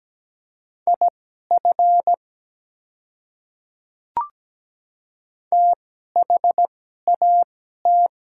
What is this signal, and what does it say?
Morse code, heard as a keyed tone, about 17 wpm, spaced as for 12 wpm, 700 Hz. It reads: IF THAT
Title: 2x Character Spacing - Top 100 Words - 17wpm